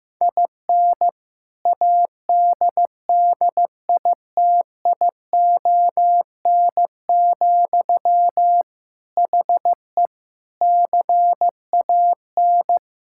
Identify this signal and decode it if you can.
Morse code, heard as a keyed tone, about 15 wpm, 700 Hz. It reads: IN ADDITION, HE CAN
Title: Sentences from Top 1000 Words - Review Part 6 - 15wpm